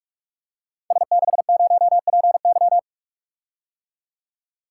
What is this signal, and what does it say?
Morse code, heard as a keyed tone, about 44 wpm, 700 Hz. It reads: S60PY
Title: Code Groups - 5 Letter:Numbers - 45wpm